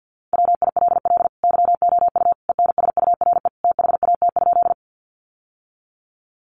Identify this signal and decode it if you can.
Morse code, heard as a keyed tone, about 48 wpm, 700 Hz. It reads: WILL YOU ENSURE THAT?
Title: Sentences from Top 1100 Words 50wpm